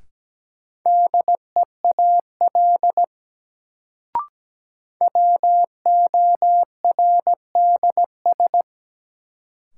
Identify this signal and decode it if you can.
Morse code, heard as a keyed tone, about 17 wpm, 700 Hz. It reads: DEAL WORDS
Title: Sets of 3 Words - New Word Drill - 17wpm